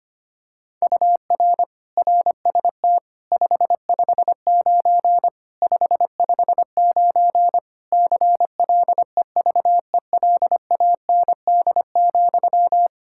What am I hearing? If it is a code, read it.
Morse code, heard as a keyed tone, about 25 wpm, 700 Hz. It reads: UR RST 559 559 CLEVELAND,
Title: K1USN SST QSO Practice 25wpm